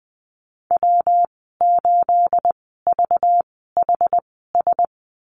Morse code, heard as a keyed tone, about 20 wpm, spaced as for 14 wpm, 700 Hz. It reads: W8VHS
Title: 231 Lesson 37g - Letter x - Call Signs 20wpm